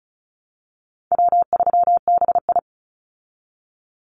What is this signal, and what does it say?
Morse code, heard as a keyed tone, about 34 wpm, 700 Hz. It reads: W3BI
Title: Call Signs Repeated 2x - 35wpm